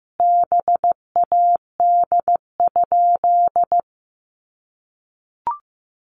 Morse code, heard as a keyed tone, about 15 wpm, 700 Hz. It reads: BAD?
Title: Sentences from Top 700 Words 15wpm